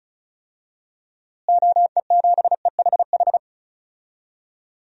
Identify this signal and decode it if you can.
Morse code, heard as a keyed tone, about 36 wpm, 700 Hz. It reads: OE7EHH
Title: DXCC-100 Call Signs - 6 Characters - Repeated 3x - 35wpm